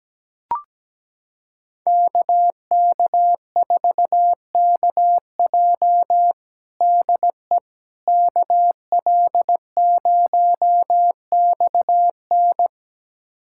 Morse code, heard as a keyed tone, about 17 wpm, 700 Hz. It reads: KK4KJ DE KL0XN
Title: USA Callsign DE Callsign - 5 Characters - 17wpm